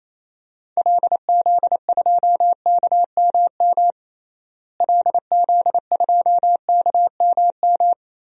Morse code, heard as a keyed tone, about 28 wpm, 700 Hz. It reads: LZ2KMM LZ2KMM
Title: DXCC-100 Call Signs - 6 Characters - Repeated 2x - 28wpm